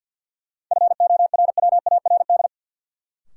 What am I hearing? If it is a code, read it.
Morse code, heard as a keyed tone, about 53 wpm, 700 Hz. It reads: FORWARD
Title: Sentences from Top 1300 Words - New Word Drill - 50wpm